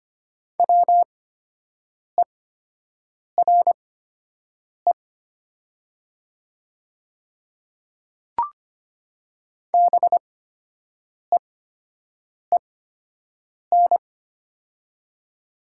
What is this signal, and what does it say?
Morse code, heard as a keyed tone, about 25 wpm, spaced as for 7 wpm, 700 Hz. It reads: WERE BEEN
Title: 8x Character Spacing - Top 100 Words - 25wpm